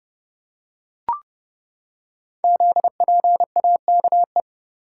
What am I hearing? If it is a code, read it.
Morse code, heard as a keyed tone, about 30 wpm, 700 Hz. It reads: ZPAKE
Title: Code Groups - 5 Letter:Numbers - 30wpm